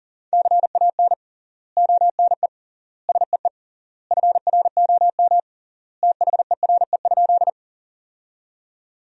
Morse code, heard as a keyed tone, about 40 wpm, 700 Hz. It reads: CAN ONE SEE FROM THERE?
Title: 3x Word-Spacing - Sentences from Top 100 Words - 40wpm